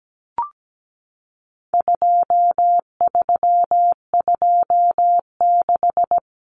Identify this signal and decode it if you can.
Morse code, heard as a keyed tone, about 17 wpm, 700 Hz. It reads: 2326
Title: Common Cut Numbers - 4-Digit 17wpm